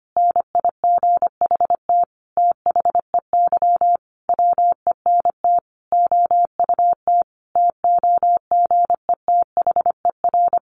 Morse code, heard as a keyed tone, about 25 wpm, 700 Hz. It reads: NIGHT THEY WENT OUT TOGETHER
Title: Sentences from Top 600 Words 25wpm